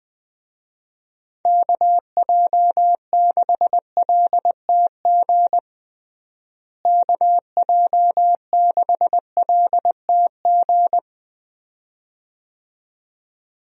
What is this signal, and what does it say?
Morse code, heard as a keyed tone, about 20 wpm, 700 Hz. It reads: KJ6LTG KJ6LTG
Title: USA Call Signs - 6 Characters - Repeated 2x - 20wpm